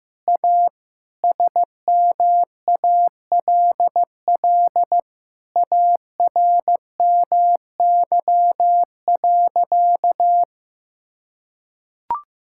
Morse code, heard as a keyed tone, about 14 wpm, 700 Hz. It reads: A SMALL ARMY.
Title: Sentences from Top 1000 Words - Review Part 6 - 15wpm